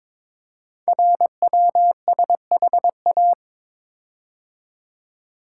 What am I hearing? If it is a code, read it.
Morse code, heard as a keyed tone, about 22 wpm, 700 Hz. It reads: RWSHA
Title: Code Groups - 5 Letters - 22wpm